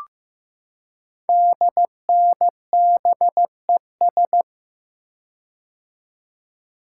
Morse code, heard as a keyed tone, about 15 wpm, 700 Hz. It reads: DNBES
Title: Code Groups - 5 Letters - 15wpm